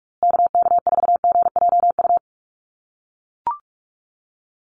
Morse code, heard as a keyed tone, about 44 wpm, 700 Hz. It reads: KK4GJU
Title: USA Call Signs Repeated 3x 45wpm